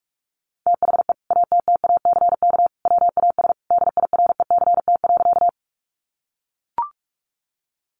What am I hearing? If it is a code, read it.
Morse code, heard as a keyed tone, about 44 wpm, 700 Hz. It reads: THE ATTACK WAS DIRECT.
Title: Sentences from Top 1500 Words 45wpm